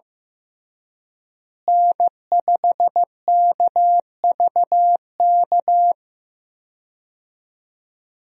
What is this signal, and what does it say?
Morse code, heard as a keyed tone, about 15 wpm, 700 Hz. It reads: N5KVK
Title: USA Call Signs - 5 Characters - Repeated 3x - 15wpm